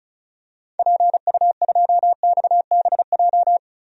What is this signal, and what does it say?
Morse code, heard as a keyed tone, about 36 wpm, 700 Hz. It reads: PU2XBJ
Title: DXCC-100 Call Signs - 6 Characters - Repeated 2x - 35wpm